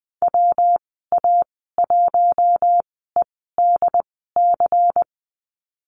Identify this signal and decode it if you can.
Morse code, heard as a keyed tone, about 20 wpm, spaced as for 15 wpm, 700 Hz. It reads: WA1EDC
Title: 193 Lesson 31g - Letter v - Call Signs 20wpm